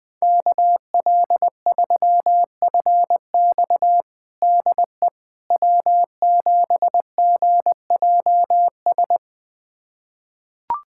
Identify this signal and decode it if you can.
Morse code, heard as a keyed tone, about 20 wpm, 700 Hz. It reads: KL3FX DE W7GJS
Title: USA Callsign DE Callsign - 5 Characters - 20wpm